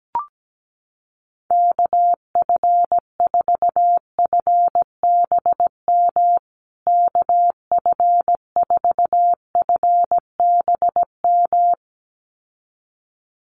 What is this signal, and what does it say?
Morse code, heard as a keyed tone, about 17 wpm, 700 Hz. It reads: KF4FBM KF4FBM
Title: USA Call Signs Repeated 2x 17wpm